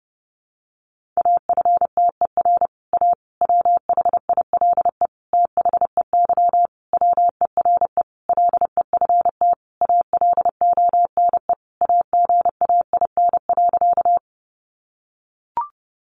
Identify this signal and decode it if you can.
Morse code, heard as a keyed tone, about 30 wpm, 700 Hz. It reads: AFTER A WHILE THEY WERE LEFT ALONE AGAIN.
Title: Sentences from Top 1000 Words 30wpm